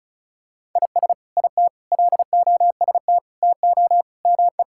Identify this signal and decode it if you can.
Morse code, heard as a keyed tone, about 36 wpm, 700 Hz. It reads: IS IT LOST TO ME
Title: Sentences from Top 1600 Words 35wpm